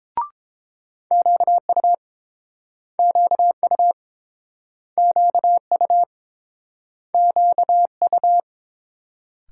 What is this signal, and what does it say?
Morse code, heard as a keyed tone, about 30 wpm, 700 Hz. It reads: QU QU QU QU